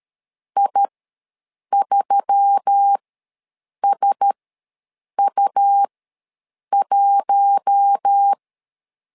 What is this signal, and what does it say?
Morse code, heard as a keyed tone, about 13 wpm, spaced as for 7 wpm, 800 Hz. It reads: I3SU1
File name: LEZIONI / 4 Q1S3 ws 12wpm 2038